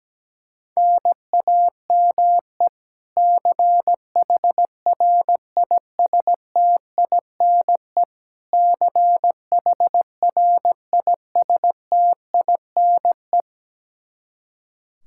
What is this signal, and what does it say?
Morse code, heard as a keyed tone, about 17 wpm, 700 Hz. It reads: NAME CHRISTINE CHRISTINE
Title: Practice QSO - Exchange 1 - Beginning - 17wpm